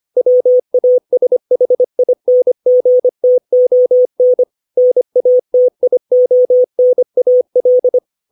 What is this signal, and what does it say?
Morse code, heard as a keyed tone, about 25 wpm, 500 Hz. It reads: WASHINGTON NATIONAL
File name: 632 words / 632 Words 25WPM